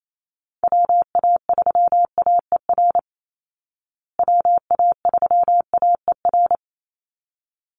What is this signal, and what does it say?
Morse code, heard as a keyed tone, about 28 wpm, 700 Hz. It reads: WA3AER WA3AER
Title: Call Signs Repeated 3x - 28wpm